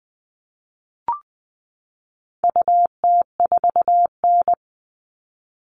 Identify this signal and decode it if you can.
Morse code, heard as a keyed tone, about 20 wpm, 700 Hz. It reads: UT4N
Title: All Cut Numbers - 4-Digit 20wpm